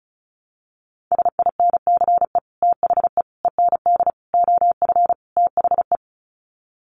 Morse code, heard as a keyed tone, about 34 wpm, 700 Hz. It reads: SINCE THE END OF THE
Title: Sets of 5 Words Encore 35wpm